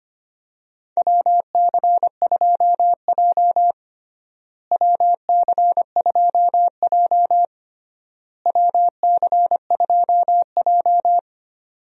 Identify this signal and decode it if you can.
Morse code, heard as a keyed tone, about 25 wpm, 700 Hz. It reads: WC2J WC2J WC2J
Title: USA Call Signs - 4 Characters - Repeated 3x - 25wpm